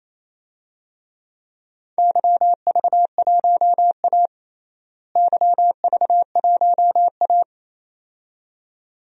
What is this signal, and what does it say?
Morse code, heard as a keyed tone, about 28 wpm, 700 Hz. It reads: YV1A YV1A
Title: DXCC-100 Call Signs - 4 Characters - Repeated 2x - 28wpm